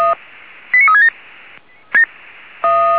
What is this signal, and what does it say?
Five-tone selective call (ZVEI), heard as quick runs of five tones, tones of about 70 ms. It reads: 89367
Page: SDR/ZVEI